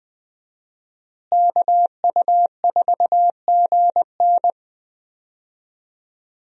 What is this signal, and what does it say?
Morse code, heard as a keyed tone, about 20 wpm, 700 Hz. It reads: KU4GN
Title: USA Call Signs - 5 Characters - Repeated 3x - 20wpm